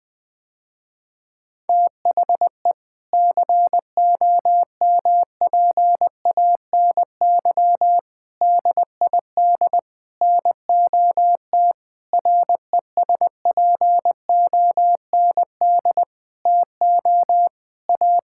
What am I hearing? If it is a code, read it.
Morse code, heard as a keyed tone, about 20 wpm, 700 Hz. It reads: THE COMPANY DID NOT RESPOND TO A
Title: Sentences with 8-Word Phrases - Spoken Beginning - 20wpm